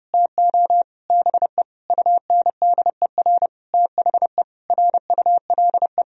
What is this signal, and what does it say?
Morse code, heard as a keyed tone, about 30 wpm, 700 Hz. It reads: TO BE UNDER THE RULE
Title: Sentences from Top 1000 Words - Review Part 3 - 30wpm